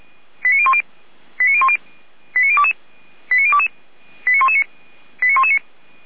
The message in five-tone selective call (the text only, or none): 89010 89010 89022 89022 89119 89119